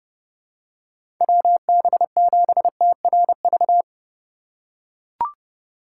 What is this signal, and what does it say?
Morse code, heard as a keyed tone, about 30 wpm, 700 Hz. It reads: WB7TRV
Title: USA Call Signs - 2x Character Spacing - 30wpm